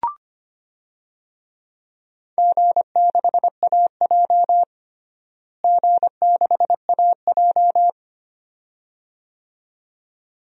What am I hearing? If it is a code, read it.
Morse code, heard as a keyed tone, about 25 wpm, 700 Hz. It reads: G6AJ G6AJ
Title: DXCC-100 Call Signs - 4 Characters - Repeated 2x - 25wpm